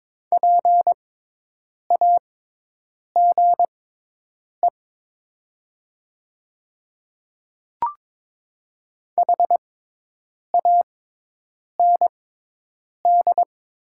Morse code, heard as a keyed tone, about 22 wpm, spaced as for 8 wpm, 700 Hz. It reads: PAGE HAND
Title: 6x Character Spacing - Top 300 Words - 22wpm